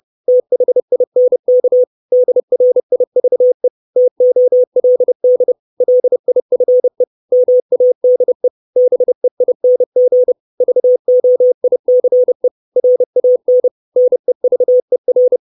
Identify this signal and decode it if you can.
Morse code, heard as a keyed tone, about 30 wpm, 500 Hz. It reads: THINK DRIVE TOLD LIFE MADE BEING VOICE RAN NEVER
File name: 632 Words 30WPM